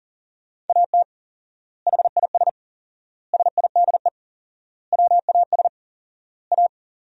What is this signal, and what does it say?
Morse code, heard as a keyed tone, about 40 wpm, 700 Hz. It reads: AT HIS SIDE WAS A
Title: 4x Word-Spacing - Sentences from Top 500 Words - 40wpm